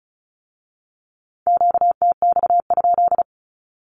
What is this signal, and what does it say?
Morse code, heard as a keyed tone, about 36 wpm, 700 Hz. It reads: QTX?
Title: Q-Codes with Spoken Meaning - 35wpm